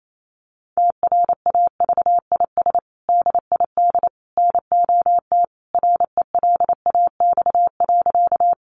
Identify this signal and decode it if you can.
Morse code, heard as a keyed tone, about 28 wpm, 700 Hz. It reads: TRAVIS DID NOT RELAX.
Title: A Sound Of Thunder 28wpm